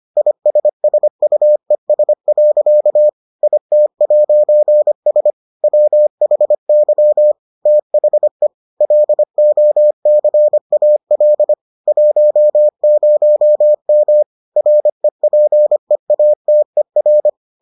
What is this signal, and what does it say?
Morse code, heard as a keyed tone, about 25 wpm, 600 Hz. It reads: ISSUES. IT'S WHY THE LOCAL 10M REPEATER